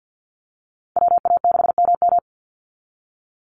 Morse code, heard as a keyed tone, about 48 wpm, 700 Hz. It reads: WA6MM
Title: NA SOTA Activator Callsigns 50wpm